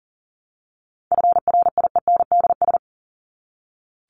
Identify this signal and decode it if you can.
Morse code, heard as a keyed tone, about 40 wpm, 700 Hz. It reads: FRIENDS
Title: ES:S Suffix - Top 100 Words with Base Words 40wpm